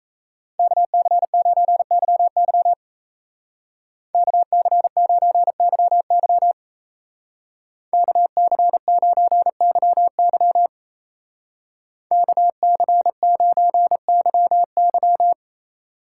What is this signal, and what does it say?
Morse code, heard as a keyed tone, about 33 wpm, 700 Hz. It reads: KC9YY KC9YY KC9YY KC9YY
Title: Speed Racing - USA 2x2 Call Signs 28wpm